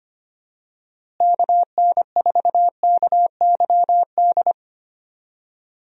Morse code, heard as a keyed tone, about 25 wpm, 700 Hz. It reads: KN4KYD
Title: USA Call Signs - 6 Characters - Repeated 2x - 25wpm